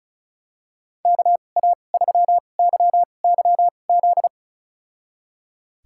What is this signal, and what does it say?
Morse code, heard as a keyed tone, about 36 wpm, spaced as for 25 wpm, 700 Hz. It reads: KA3YYZ
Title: USA Call Signs - 6 Characters - 2x Character Spacing - 35wpm